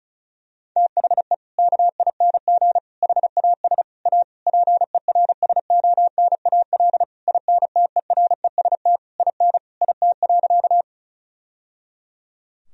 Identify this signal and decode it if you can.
Morse code, heard as a keyed tone, about 34 wpm, 700 Hz. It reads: THE KING HAS A PERSONAL INTEREST IN IT.
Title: Sentences from Top 1000 Words - Review Part 4 - 35wpm